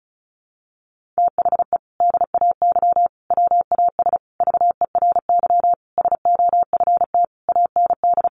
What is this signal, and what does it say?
Morse code, heard as a keyed tone, about 34 wpm, 700 Hz. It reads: THE DAY WAS VERY SOFT AND
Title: Sentences from Top 1500 Words 35wpm